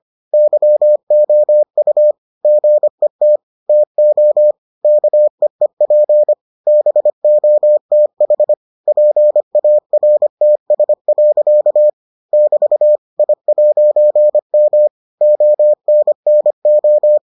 Morse code, heard as a keyed tone, about 25 wpm, 600 Hz. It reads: YOU GET TO KEEP BOTH PARTS. <BT> I'M ONNO